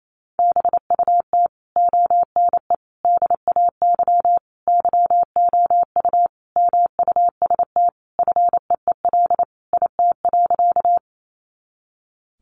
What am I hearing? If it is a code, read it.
Morse code, heard as a keyed tone, about 28 wpm, 700 Hz. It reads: BUT ONE DAY YOU MUST FEEL IT.
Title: Sentences from Top 400 Words 28wpm